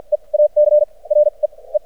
Morse code, heard as a keyed tone, about 56 wpm, spaced as for 34 wpm, 600 Hz. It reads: E ACRE E
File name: archived / CRSnet